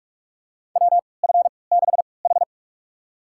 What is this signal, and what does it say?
Morse code, heard as a keyed tone, about 45 wpm, spaced as for 25 wpm, 700 Hz. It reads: WF6H
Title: USA Call Signs - 4 Characters - 3x Character Spacing - 45wpm